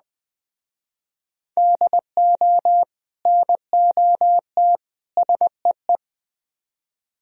Morse code, heard as a keyed tone, about 21 wpm, 700 Hz. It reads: DO NOT SEE
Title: Speed-Racing - Sets of 3 Words 17wpm